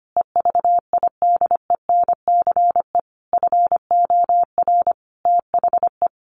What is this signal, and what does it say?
Morse code, heard as a keyed tone, about 25 wpm, 700 Hz. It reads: EVIDENCE FOR THE
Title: Sentences from Top 1300 Words 25wpm